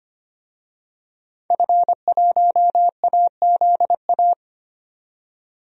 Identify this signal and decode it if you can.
Morse code, heard as a keyed tone, about 25 wpm, 700 Hz. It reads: F1AZA
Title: DXCC-100 Call Signs - 5 Characters - 25wpm